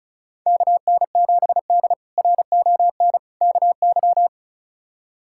Morse code, heard as a keyed tone, about 34 wpm, 700 Hz. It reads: KN7D RON KY
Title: K1USN SST QSO Practice - 35wpm